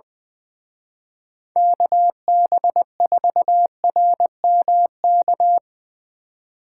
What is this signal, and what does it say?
Morse code, heard as a keyed tone, about 22 wpm, 700 Hz. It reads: KB4RMK
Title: Speed Racing - USA 2x3 Call Signs 20wpm